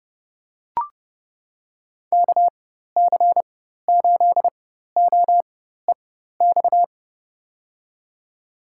Morse code, heard as a keyed tone, about 29 wpm, spaced as for 14 wpm, 700 Hz. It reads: KC8OEX
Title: USA Call Signs - 6 Characters - 4x Character Spacing - 30wpm